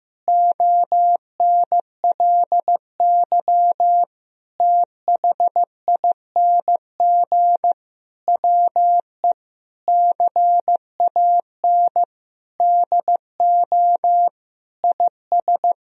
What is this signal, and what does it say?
Morse code, heard as a keyed tone, about 15 wpm, 700 Hz. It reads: ONLY THING WE CAN DO IS
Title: Sentences from Top 200 Words and Abbreviations 15wpm